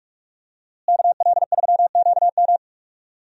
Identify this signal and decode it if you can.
Morse code, heard as a keyed tone, about 44 wpm, 700 Hz. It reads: KL3QM
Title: USA Call Signs - 5 Characters - Repeated 2x - 45wpm